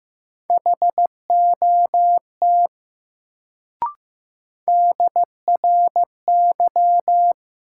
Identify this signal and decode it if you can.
Morse code, heard as a keyed tone, about 15 wpm, 700 Hz. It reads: HOT DRY
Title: Sentences from Top 500 Words - New Word Drill - 15wpm